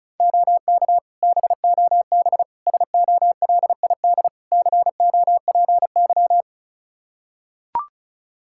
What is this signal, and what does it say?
Morse code, heard as a keyed tone, about 34 wpm, 700 Hz. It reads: OK BOB SOLID COPY
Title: Practice QSO - Exchange 3 - More Info - 35wpm